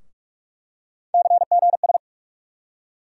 Morse code, heard as a keyed tone, about 46 wpm, 700 Hz. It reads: CGS
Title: USA Call Sign Postfixes - 3 Letters 45wpm